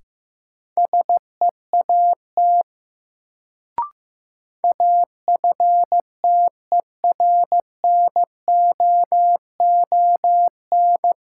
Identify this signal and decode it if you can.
Morse code, heard as a keyed tone, about 15 wpm, 700 Hz. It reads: SEAT AFTERNOON
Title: Sentences from Top 1500 Words - New Word Drill Alt - 15wpm